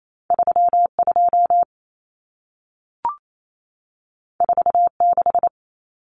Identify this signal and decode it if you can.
Morse code, heard as a keyed tone, about 28 wpm, 700 Hz. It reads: 32 46
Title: Numbers - 2-Digit 28wpm